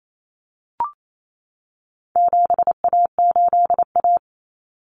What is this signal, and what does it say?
Morse code, heard as a keyed tone, about 28 wpm, 700 Hz. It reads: 7A8A
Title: Common Cut Numbers - 4-Digit 28wpm